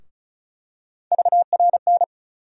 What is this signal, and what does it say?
Morse code, heard as a keyed tone, about 34 wpm, 700 Hz. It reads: VRN
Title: USA Call Sign Postfixes - 3 Letters 35wpm